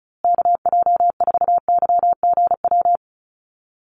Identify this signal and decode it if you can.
Morse code, heard as a keyed tone, about 34 wpm, 700 Hz. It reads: KJ4YGW
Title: USA Call Signs Repeated 3x 35wpm